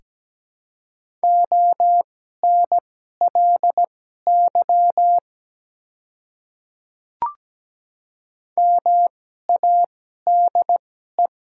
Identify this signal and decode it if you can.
Morse code, heard as a keyed tone, about 17 wpm, spaced as for 12 wpm, 700 Hz. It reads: ONLY MADE
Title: 2x Character Spacing - Top 300 Words - 17wpm